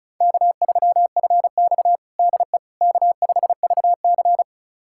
Morse code, heard as a keyed tone, about 36 wpm, 700 Hz. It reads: K3FX DE K5VC
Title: USA Callsign DE Callsign - 4 Characters - 35wpm